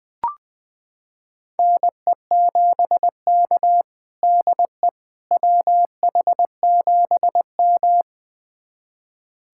Morse code, heard as a keyed tone, about 21 wpm, 700 Hz. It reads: NE7K DE WH7M
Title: USA Callsign DE Callsign - 4 Characters - 20wpm